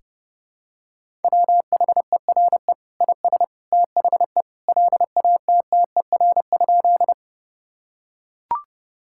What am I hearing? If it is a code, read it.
Morse code, heard as a keyed tone, about 30 wpm, 700 Hz. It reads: WHERE IS THE LATTER?
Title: Sentences from Top 1600 Words 30wpm